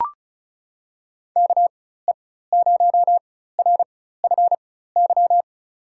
Morse code, heard as a keyed tone, about 36 wpm, spaced as for 16 wpm, 700 Hz. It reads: KE0RFY
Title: USA Call Signs - 6 Characters - 4x Character Spacing - 35wpm